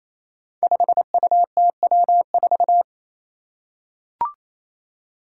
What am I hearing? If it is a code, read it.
Morse code, heard as a keyed tone, about 28 wpm, 700 Hz. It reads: 5UTW4
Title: Five Challenging Letter-Number Combinations 28wpm